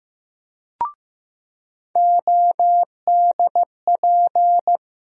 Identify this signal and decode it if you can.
Morse code, heard as a keyed tone, about 15 wpm, 700 Hz. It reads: ODP
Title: USA Call Sign Postfixes - 3 Letters 15wpm